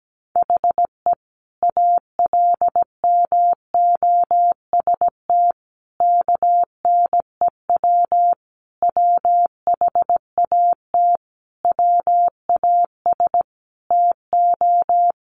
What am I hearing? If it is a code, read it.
Morse code, heard as a keyed tone, about 17 wpm, 700 Hz. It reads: HE ALMOST KNEW WHAT WAS TO
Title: Sentences from Top 700 Words 17wpm